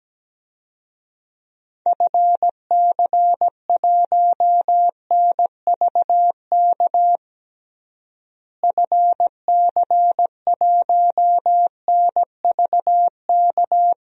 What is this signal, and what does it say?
Morse code, heard as a keyed tone, about 17 wpm, 700 Hz. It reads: FC1NVK FC1NVK
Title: DXCC-100 Call Signs - 6 Characters - Repeated 3x - 17wpm